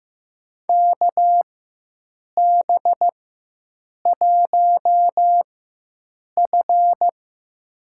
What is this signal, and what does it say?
Morse code, heard as a keyed tone, about 15 wpm, spaced as for 7 wpm, 700 Hz. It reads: KB1F
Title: USA Call Signs - 4 Characters - 4x Character Spacing - 15wpm